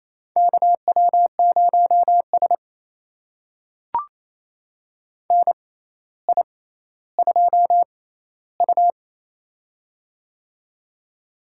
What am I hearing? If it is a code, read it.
Morse code, heard as a keyed tone, about 28 wpm, spaced as for 14 wpm, 700 Hz. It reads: KW0S NI2U
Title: USA Call Signs - 4 Characters - 6x Character Spacing - 28wpm